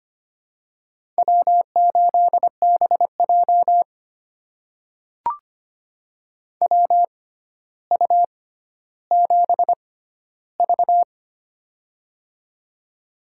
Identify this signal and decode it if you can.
Morse code, heard as a keyed tone, about 25 wpm, spaced as for 13 wpm, 700 Hz. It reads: W8BJ WU7V
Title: USA Call Signs - 6x Character Spacing - 25wpm